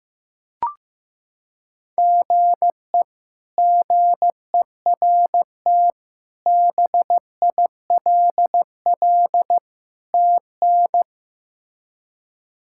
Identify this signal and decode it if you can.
Morse code, heard as a keyed tone, about 15 wpm, 700 Hz. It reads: GE GERT BILL TN
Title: K1USN SST QSO Practice - 15wpm